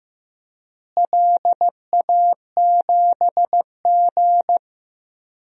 Morse code, heard as a keyed tone, about 15 wpm, 700 Hz. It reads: LA7G
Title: DXCC-100 Call Signs - 4 Characters - Repeated 2x - 15wpm